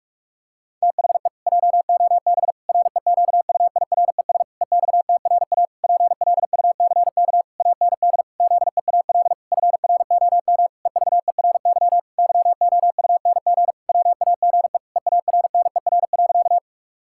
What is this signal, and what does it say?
Morse code, heard as a keyed tone, about 46 wpm, 700 Hz. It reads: THE JOB REQUIRES EXTRA PLUCK AND ZEAL FROM EVERY YOUNG WAGE EARNER.
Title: Pangram Challenge v2 - 45wpm